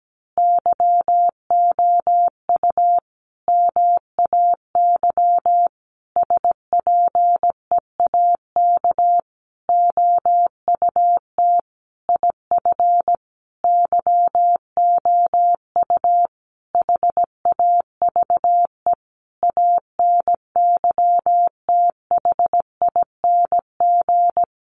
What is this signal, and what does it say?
Morse code, heard as a keyed tone, about 17 wpm, 700 Hz. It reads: YOU MAY SPEAK OUT IF YOU HAVE ANYTHING
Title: Sentences from Top 1300 Words 17wpm